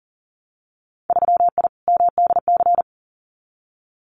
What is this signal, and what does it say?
Morse code, heard as a keyed tone, about 38 wpm, 700 Hz. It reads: 3I MDC
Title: Winter Field Day Exchanges Encore 40wpm